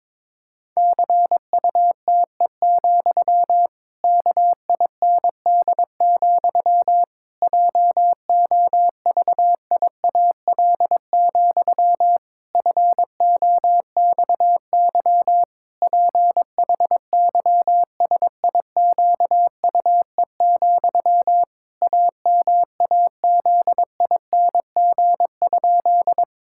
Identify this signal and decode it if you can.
Morse code, heard as a keyed tone, about 22 wpm, 700 Hz. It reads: CUTE, KIND, JOVIAL, FOXY PHYSIQUE, AMAZING?
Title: Pangram Challenge v2 - 22wpm